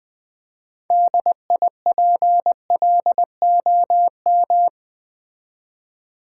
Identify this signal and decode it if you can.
Morse code, heard as a keyed tone, about 20 wpm, 700 Hz. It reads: DIPLOM
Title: Build A Word - 3 Letters At A Time - 8-Letter Words - 20wpm